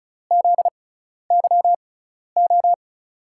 Morse code, heard as a keyed tone, about 35 wpm, spaced as for 12 wpm, 700 Hz. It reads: ZYO
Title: USA Call Signs - 6 Characters - 6x Character Spacing - 35wpm